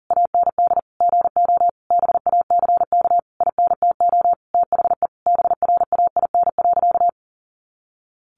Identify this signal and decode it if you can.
Morse code, heard as a keyed tone, about 40 wpm, 700 Hz. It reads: AND GO BACK INTO THE BRAIN.
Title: A Sound Of Thunder 40wpm